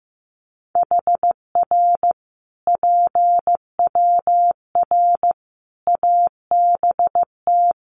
Morse code, heard as a keyed tone, about 15 wpm, 700 Hz. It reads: HR PWR ABT
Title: Common QSO Phrases 15wpm